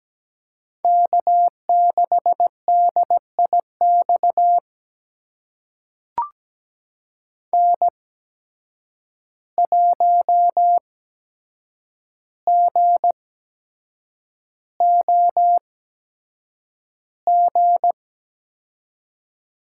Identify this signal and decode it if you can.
Morse code, heard as a keyed tone, about 17 wpm, spaced as for 7 wpm, 700 Hz. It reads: K6DIX N1GOG
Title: USA Call Signs - 5 Characters - 8x Character Spacing - 17wpm